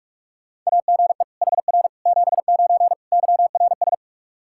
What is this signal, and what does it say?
Morse code, heard as a keyed tone, about 46 wpm, 700 Hz. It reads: AGE HR 79 YRS
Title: Practice QSO - Exchange 3 - More Info - 45wpm